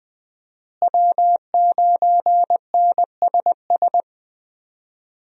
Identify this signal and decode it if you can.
Morse code, heard as a keyed tone, about 20 wpm, 700 Hz. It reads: W9NSS
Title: USA Call Signs - 5 Characters - Repeated 3x - 20wpm